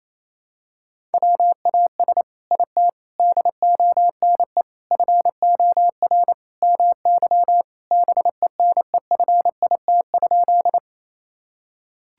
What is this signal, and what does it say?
Morse code, heard as a keyed tone, about 28 wpm, 700 Hz. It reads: WAS IT DONE FOR MY BENEFIT?
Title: Sentences from Top 1000 Words - Review Part 6 - 28wpm